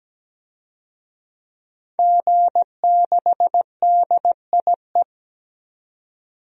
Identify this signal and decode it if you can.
Morse code, heard as a keyed tone, about 17 wpm, 700 Hz. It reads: G6DIE
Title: DXCC-100 Call Signs - 5 Characters - 17wpm